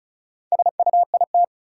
Morse code, heard as a keyed tone, about 36 wpm, 700 Hz. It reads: SUIT
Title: Sentences from Top 1600 Words - New Word Drill - 35wpm